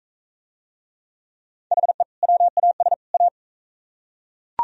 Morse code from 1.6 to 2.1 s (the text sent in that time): HE